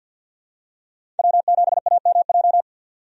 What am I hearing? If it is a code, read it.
Morse code, heard as a keyed tone, about 48 wpm, 700 Hz. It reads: W7AMJ
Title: USA Call Signs - 5 Characters - 50wpm